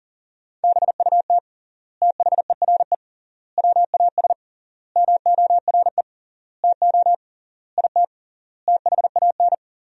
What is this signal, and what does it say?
Morse code, heard as a keyed tone, about 37 wpm, 700 Hz. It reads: BUT THERE WAS MORE TO IT THAN
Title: 3x Word-Spacing - Sentences from Top 100 Words - 40wpm